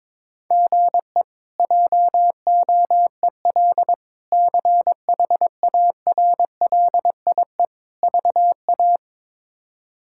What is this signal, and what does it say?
Morse code, heard as a keyed tone, about 22 wpm, 700 Hz. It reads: GE JOEL CHARLIE VA
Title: K1USN SST QSO Practice - 22wpm